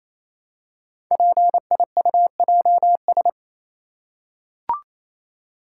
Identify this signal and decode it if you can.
Morse code, heard as a keyed tone, about 27 wpm, 700 Hz. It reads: PIUJS
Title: Code Groups - 5 Letters - 28wpm